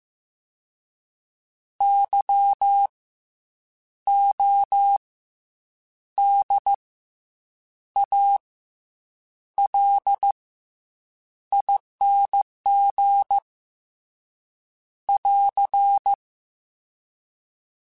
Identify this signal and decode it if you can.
Morse code, heard as a keyed tone, about 15 wpm, 800 Hz. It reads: Y O D A L ING <AR>
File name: Special INGs / Special INGs-mp3s / SlowToFaster ing @13